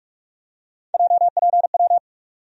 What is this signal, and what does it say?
Morse code, heard as a keyed tone, about 42 wpm, 700 Hz. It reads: JPW
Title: USA Call Sign Postfixes - 3 Letters 45wpm